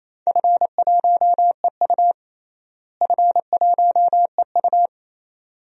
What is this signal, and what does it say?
Morse code, heard as a keyed tone, about 28 wpm, 700 Hz. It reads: F1EU F1EU
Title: DXCC-100 Call Signs - 4 Characters - Repeated 2x - 28wpm